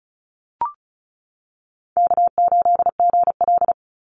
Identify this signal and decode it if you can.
Morse code, heard as a keyed tone, about 34 wpm, 700 Hz. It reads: K8GL
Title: Call Signs - 35wpm